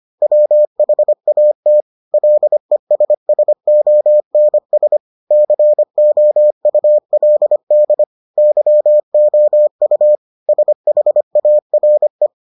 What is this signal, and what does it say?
Morse code, heard as a keyed tone, about 25 wpm, 600 Hz. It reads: WHAT LESSONS COULD YOU SHARE